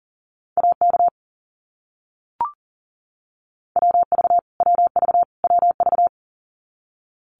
Morse code, heard as a keyed tone, about 40 wpm, 700 Hz. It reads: AK WV WV WV
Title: US States Repeated 3x 40wpm